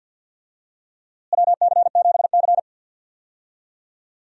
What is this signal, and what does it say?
Morse code, heard as a keyed tone, about 46 wpm, 700 Hz. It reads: WC7C
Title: USA Call Signs - 4 Characters - Repeated 3x - 50wpm